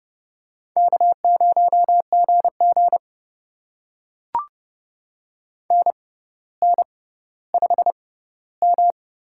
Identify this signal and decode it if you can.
Morse code, heard as a keyed tone, about 30 wpm, spaced as for 15 wpm, 700 Hz. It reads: K0GG NN5M
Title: USA Call Signs - 4 Characters - 6x Character Spacing - 30wpm